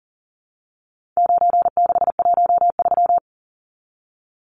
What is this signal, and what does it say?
Morse code, heard as a keyed tone, about 40 wpm, 700 Hz. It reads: 9613
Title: Numbers - 4-Digit 40wpm